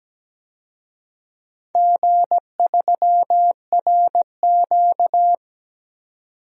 Morse code, heard as a keyed tone, about 17 wpm, 700 Hz. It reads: G3RQ
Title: DXCC-100 Call Signs - 4 Characters - 17wpm